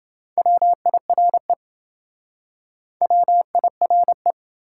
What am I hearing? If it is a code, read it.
Morse code, heard as a keyed tone, about 28 wpm, 700 Hz. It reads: WIRE WIRE